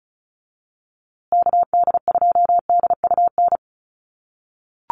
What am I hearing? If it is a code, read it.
Morse code, heard as a keyed tone, about 34 wpm, 700 Hz. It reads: KD2DUN
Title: USA Call Signs Repeated 2x 35wpm